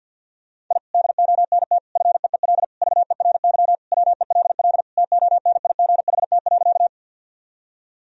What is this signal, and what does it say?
Morse code, heard as a keyed tone, about 50 wpm, 700 Hz. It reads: I DONT FEEL VERY WELL TONIGHT.
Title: Sentences from Top 1000 Words - Review Part 3 - 50wpm